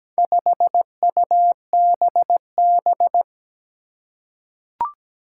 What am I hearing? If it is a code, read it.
Morse code, heard as a keyed tone, about 17 wpm, 700 Hz. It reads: 5UBB
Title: Four Challenging Letter-Number Combinations 17wpm